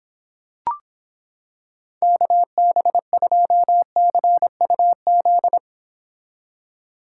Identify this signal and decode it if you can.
Morse code, heard as a keyed tone, about 26 wpm, 700 Hz. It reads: KB2CUZ